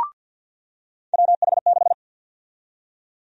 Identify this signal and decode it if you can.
Morse code, heard as a keyed tone, about 49 wpm, 700 Hz. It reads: WH6
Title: USA Call Sign Prefixes - 50wpm